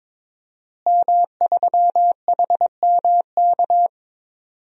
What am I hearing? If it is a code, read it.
Morse code, heard as a keyed tone, about 22 wpm, 700 Hz. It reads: M3HMK
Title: DXCC-100 Call Signs - 5 Characters - Repeated 2x - 22wpm